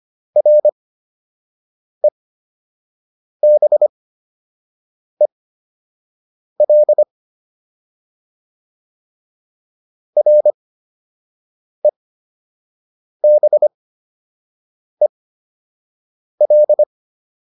Morse code, heard as a keyed tone, about 25 wpm, spaced as for 6 wpm, 600 Hz. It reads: REBEL REBEL